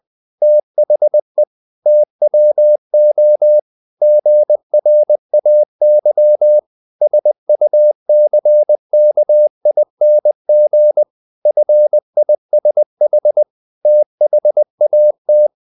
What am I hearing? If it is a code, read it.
Morse code, heard as a keyed tone, about 20 wpm, 600 Hz. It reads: THE TWO GRAY SUCKING FISH THAT
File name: the old man and the sea part 2 20wpm